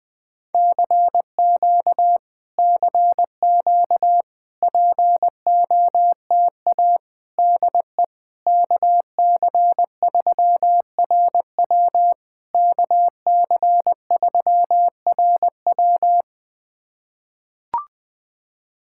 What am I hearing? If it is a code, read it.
Morse code, heard as a keyed tone, about 20 wpm, 690 Hz. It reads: CQ CQ POTA DE KC3RW KC3RW
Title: POTA QSO Practice - 20wpm